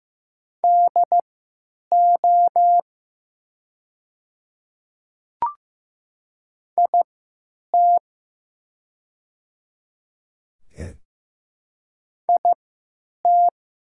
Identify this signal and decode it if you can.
Morse code, heard as a keyed tone, about 15 wpm, spaced as for 8 wpm, 700 Hz. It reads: DO IT IT
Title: 3x Character Spacing - Top 100 Words - 15wpm